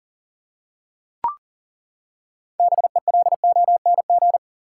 Morse code, heard as a keyed tone, about 37 wpm, 700 Hz. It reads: BELONG
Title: Six Letter Words - 40wpm